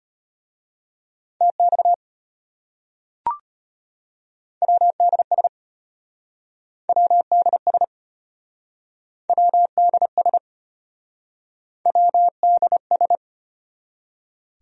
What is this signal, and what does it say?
Morse code, heard as a keyed tone, about 35 wpm, 700 Hz. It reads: TX WDS WDS WDS WDS